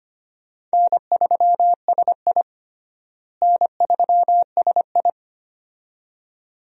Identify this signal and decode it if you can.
Morse code, heard as a keyed tone, about 25 wpm, 700 Hz. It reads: N3SI N3SI
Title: USA Call Signs - 4 Characters - Repeated 2x - 25wpm